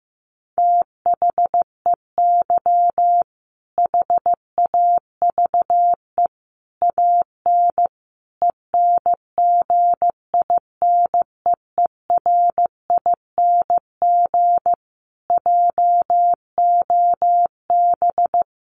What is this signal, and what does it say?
Morse code, heard as a keyed tone, about 15 wpm, 700 Hz. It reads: THEY HAVE AN ENGINEERING JOB
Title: Sentences from Top 1500 Words 15wpm